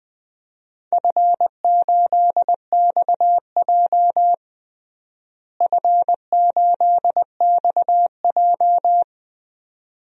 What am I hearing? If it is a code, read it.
Morse code, heard as a keyed tone, about 20 wpm, 700 Hz. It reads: F8XJ F8XJ
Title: DXCC-100 Call Signs - 4 Characters - Repeated 2x - 20wpm